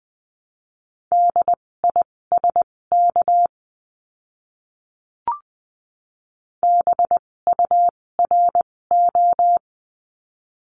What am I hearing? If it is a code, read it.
Morse code, heard as a keyed tone, about 20 wpm, spaced as for 16 wpm, 700 Hz. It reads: DISK BURO
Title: 256 Lesson 42a - Words 12-20wpm